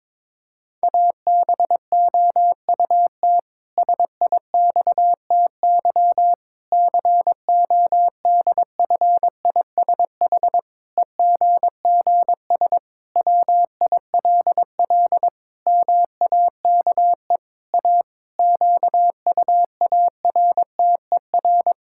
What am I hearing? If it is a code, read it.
Morse code, heard as a keyed tone, about 22 wpm, 700 Hz. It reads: ABOUT SIXTY CODFISH EGGS WILL MAKE A QUARTER
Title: Pangram Challenge v2 - 22wpm